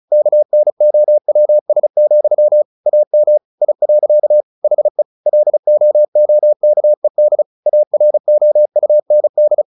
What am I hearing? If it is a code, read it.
Morse code, heard as a keyed tone, about 34 wpm, 600 Hz. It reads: KNOWS, AM I. HE LOOKED AROUND
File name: the old man and the sea / the old man and the sea 1 35wpm